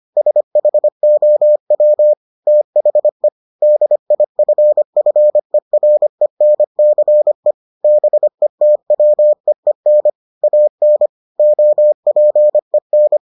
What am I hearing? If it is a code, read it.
Morse code, heard as a keyed tone, about 25 wpm, 600 Hz. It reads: SHOW THE DIFFERENCE BETWEEN AN OPEN